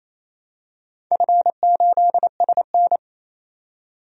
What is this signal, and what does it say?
Morse code, heard as a keyed tone, about 27 wpm, 700 Hz. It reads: F8SN
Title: DXCC-100 Call Signs - 4 Characters - 28wpm